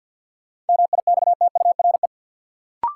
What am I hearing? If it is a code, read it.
Morse code, heard as a keyed tone, about 52 wpm, 700 Hz. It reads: MIXTURE
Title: Sentences from Top 1500 Words - New Word Drill - 50wpm